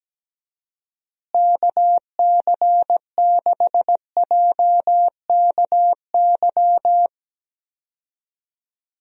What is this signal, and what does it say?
Morse code, heard as a keyed tone, about 17 wpm, 700 Hz. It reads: KC6JKY
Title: USA Call Signs - 6 Characters - Repeated 2x - 17wpm